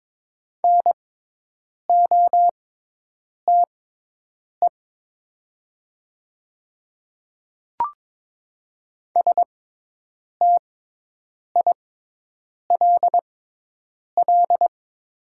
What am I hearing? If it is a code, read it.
Morse code, heard as a keyed tone, about 23 wpm, spaced as for 8 wpm, 700 Hz. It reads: NOTE STILL
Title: 6x Character Spacing - Top 500 Words - 22wpm